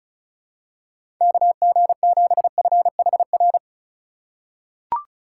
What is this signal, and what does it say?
Morse code, heard as a keyed tone, about 36 wpm, 700 Hz. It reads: KG7FHR
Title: USA Call Signs - 6 Characters - 3x Character Spacing - 35wpm